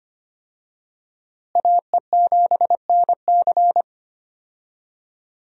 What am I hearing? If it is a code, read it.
Morse code, heard as a keyed tone, about 25 wpm, 700 Hz. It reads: AE7NC
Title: USA Call Signs - 5 Characters - Repeated 2x - 25wpm